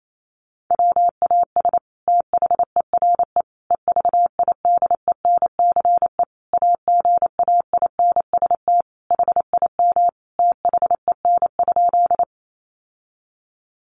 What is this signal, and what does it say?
Morse code, heard as a keyed tone, about 28 wpm, 700 Hz. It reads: WAS THERE EVIDENCE AGAINST HIM THEN?
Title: Sentences from Top 1300 Words 28wpm